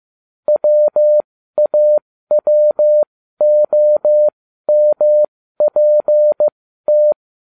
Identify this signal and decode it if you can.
Morse code, heard as a keyed tone, about 15 wpm, spaced as for 12 wpm, 600 Hz. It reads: WAWOMPT